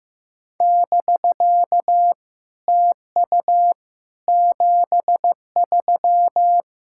Morse code, heard as a keyed tone, about 15 wpm, 700 Hz. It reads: <BK> TU 73
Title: SOTA QSO Practice 15wpm